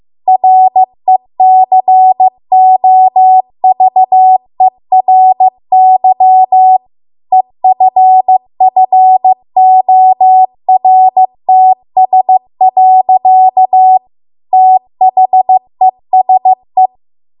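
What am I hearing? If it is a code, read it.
Morse code, heard as a keyed tone, about 15 wpm, 750 Hz. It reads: RECOVERY EFFORTS. THESE
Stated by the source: Morse code practice at 15 wpm from the national amateur radio association's code archive.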